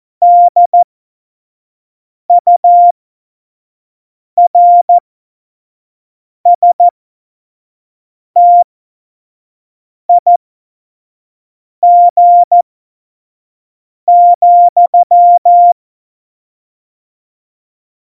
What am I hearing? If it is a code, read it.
Morse code, heard as a keyed tone, about 14 wpm, spaced as for 5 wpm, 700 Hz. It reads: DURSTIG,
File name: Morsekurs / Witze / Witz 27. August 2019, 25 F70